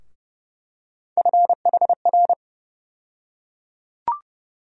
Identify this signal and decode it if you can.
Morse code, heard as a keyed tone, about 30 wpm, 700 Hz. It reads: FHR
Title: USA Call Sign Postfixes - 3 Letters 30wpm